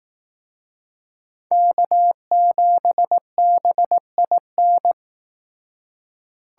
Morse code, heard as a keyed tone, about 18 wpm, 700 Hz. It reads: K7BIN
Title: Speed Racing - USA 1x3 Call Signs 15wpm